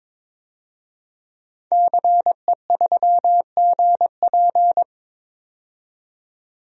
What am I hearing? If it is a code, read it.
Morse code, heard as a keyed tone, about 22 wpm, 700 Hz. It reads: CE3GP
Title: DXCC-100 Call Signs - 5 Characters - 22wpm